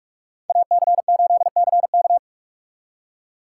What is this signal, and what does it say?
Morse code, heard as a keyed tone, about 56 wpm, 700 Hz. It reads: AC8CK
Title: Speed Racing - USA Call Signs 45wpm